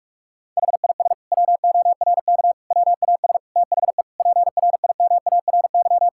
Morse code, heard as a keyed tone, about 46 wpm, 690 Hz. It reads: HIS WORK WAS THE PRIMARY
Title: Sentences from Top 1000 Words - Review Part 5 - 45wpm